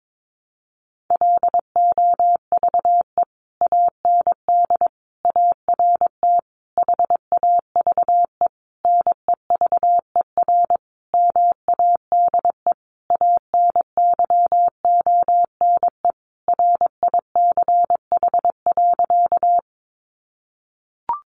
Morse code, heard as a keyed tone, about 22 wpm, 700 Hz. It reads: LOVE AND ART HAVE NEVER MADE ANYONE RICH.
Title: Sentences from Top 1100 Words 22wpm